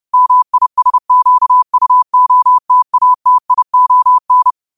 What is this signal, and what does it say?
Morse code, heard as a keyed tone, about 30 wpm, 1000 Hz. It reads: MISQUOTATION
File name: qotdfiles / 30wpm